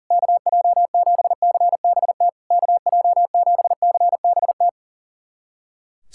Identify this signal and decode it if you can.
Morse code, heard as a keyed tone, about 40 wpm, 690 Hz. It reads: KJ7CBT KJ7CBT
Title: SOTA QSO Practice 40wpm